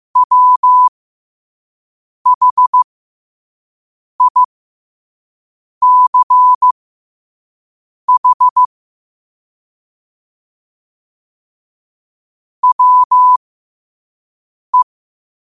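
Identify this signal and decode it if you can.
Morse code, heard as a keyed tone, about 15 wpm, spaced as for 5 wpm, 1000 Hz. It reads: WHICH WE
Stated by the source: Morse code sent at 5 wpm.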